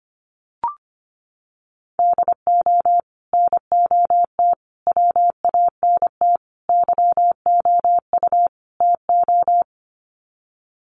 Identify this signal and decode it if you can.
Morse code, heard as a keyed tone, about 25 wpm, 700 Hz. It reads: DO NOT WANT YOU TO
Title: Sets of 5 Words 25wpm